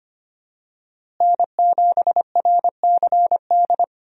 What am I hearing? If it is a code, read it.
Morse code, heard as a keyed tone, about 25 wpm, 700 Hz. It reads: N7RCD
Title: USA Call Signs - 5 Characters - Repeated 3x - 25wpm